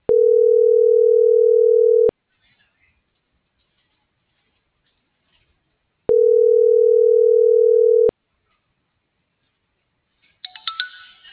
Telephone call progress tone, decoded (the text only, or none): ringback tone